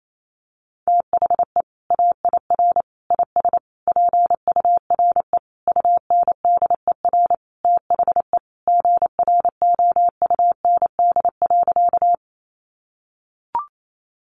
Morse code, heard as a keyed tone, about 28 wpm, 700 Hz. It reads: THE AIR IS PURE UNDER THE GROUND.
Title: Sentences from Top 1500 Words 28wpm